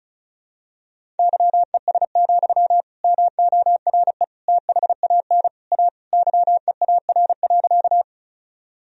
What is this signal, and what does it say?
Morse code, heard as a keyed tone, about 34 wpm, 700 Hz. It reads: YES, MORE THAN A YEAR.
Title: Sentences from Top 1000 Words - Review Part 1 - 35wpm